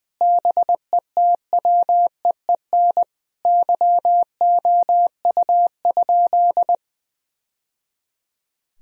Morse code, heard as a keyed tone, about 20 wpm, 700 Hz. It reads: BETWEEN YOU?
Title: Sentences from Top 1000 Words - Review Part 4 - 20wpm